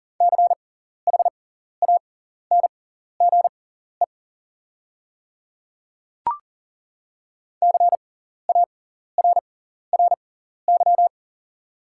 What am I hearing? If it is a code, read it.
Morse code, heard as a keyed tone, about 38 wpm, spaced as for 14 wpm, 690 Hz. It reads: CHANGE CARRY
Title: 6x Character Spacing - Top 300 Words - 40wpm